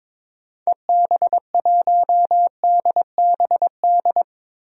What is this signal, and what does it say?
Morse code, heard as a keyed tone, about 22 wpm, 700 Hz. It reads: EB1DBD
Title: DXCC-100 Call Signs - 6 Characters - Repeated 3x - 22wpm